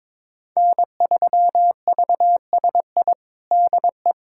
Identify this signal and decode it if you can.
Morse code, heard as a keyed tone, about 22 wpm, 700 Hz. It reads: N3VSI DE
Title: USA Callsign DE Callsign - 5 Characters - 22wpm